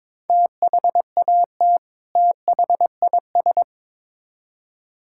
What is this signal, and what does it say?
Morse code, heard as a keyed tone, about 22 wpm, 700 Hz. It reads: THAT THIS
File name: Build A Phrase - 2-Word Phrases - 22wpm